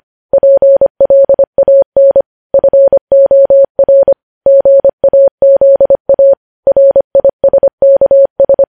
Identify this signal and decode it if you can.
Morse code, heard as a keyed tone, about 25 wpm, 550 Hz. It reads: PLAN FOR GAZA RISKS
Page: news headlines in morse code at 25 wpm